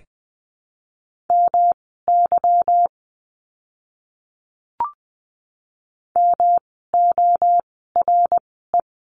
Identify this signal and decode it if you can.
Morse code, heard as a keyed tone, about 20 wpm, spaced as for 15 wpm, 700 Hz. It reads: MY MORE
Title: 244 Lesson 39f - Letter z - Words 20wpm